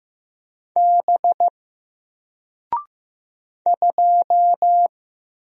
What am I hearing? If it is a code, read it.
Morse code, heard as a keyed tone, about 15 wpm, 700 Hz. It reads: B 2
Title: Single Character (Letter:Number:Punctuation:Prosign) Alt 15wpm